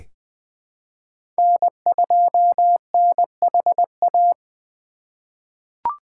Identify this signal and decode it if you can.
Morse code, heard as a keyed tone, about 20 wpm, 700 Hz. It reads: N2NHA
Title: USA Call Signs - 5 Characters - 6x Character Spacing - 20wpm